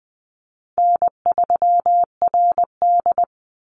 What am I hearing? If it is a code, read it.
Morse code, heard as a keyed tone, about 20 wpm, 700 Hz. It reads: N3RD
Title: Easy Call Signs - 20wpm